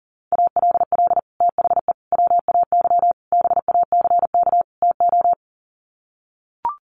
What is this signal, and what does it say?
Morse code, heard as a keyed tone, about 40 wpm, 700 Hz. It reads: ALL THE WAY BACK TO
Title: Sets of 5 Words Encore 40wpm